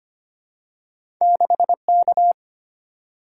Morse code, heard as a keyed tone, about 25 wpm, 690 Hz. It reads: 6K
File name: Two Letter-Number Combinations - 25wpm